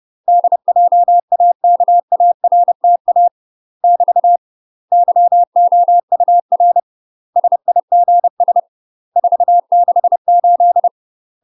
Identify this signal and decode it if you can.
Morse code, heard as a keyed tone, about 30 wpm, 700 Hz. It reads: DJAKARTA <BT> YOUR SIGS 468
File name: UK MASTER QSOs TRAINING / 30WPM / UK MASTER QSOs 0023